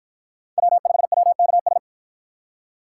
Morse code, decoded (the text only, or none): W5WKS